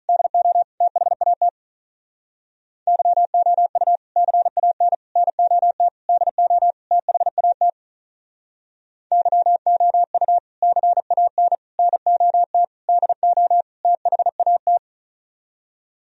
Morse code, heard as a keyed tone, about 47 wpm, 700 Hz. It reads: DO THAT YOU CAN NOT DO THAT YOU CAN NOT DO THAT